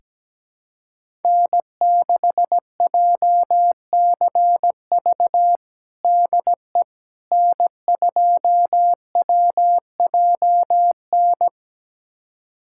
Morse code, heard as a keyed tone, about 17 wpm, 700 Hz. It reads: N6JCV DE N2WJN
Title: USA Callsign DE Callsign - 5 Characters - 17wpm